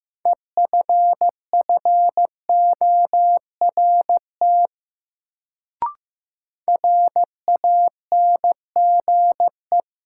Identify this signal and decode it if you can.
Morse code, heard as a keyed tone, about 15 wpm, 700 Hz. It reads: EFFORT RANGE
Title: Sets of 4 Words - New Word Drill - 15wpm